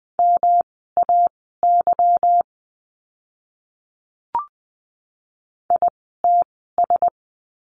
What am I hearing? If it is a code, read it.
Morse code, heard as a keyed tone, about 20 wpm, spaced as for 14 wpm, 700 Hz. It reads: MAY ITS
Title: 230 Lesson 37f - Letter x - Words 20wpm